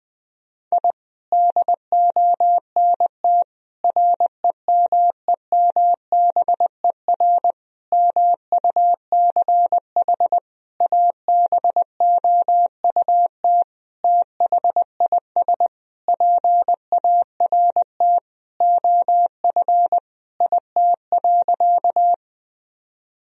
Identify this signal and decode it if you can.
Morse code, heard as a keyed tone, about 20 wpm, 700 Hz. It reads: I DONT REMEMBER MUCH ABOUT THIS PART OF IT.
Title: Sentences from Top 1000 Words - Review Part 1 - 20wpm